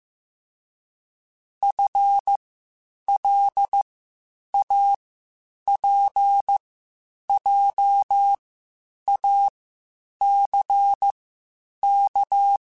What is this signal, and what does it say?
Morse code, heard as a keyed tone, about 15 wpm, spaced as for 8 wpm, 800 Hz. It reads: FLAPJACK